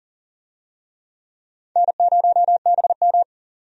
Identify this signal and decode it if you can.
Morse code, heard as a keyed tone, about 40 wpm, 700 Hz. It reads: N0BM
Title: USA Call Signs - 4 Characters - 40wpm